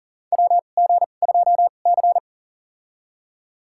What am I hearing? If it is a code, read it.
Morse code, heard as a keyed tone, about 40 wpm, spaced as for 29 wpm, 700 Hz. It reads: WG2C
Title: USA Call Signs - 2x Character Spacing - 40wpm